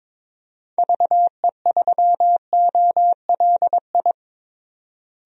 Morse code, heard as a keyed tone, about 22 wpm, 700 Hz. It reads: VE3OLI
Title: DXCC-100 Call Signs - 6 Characters - Repeated 2x - 22wpm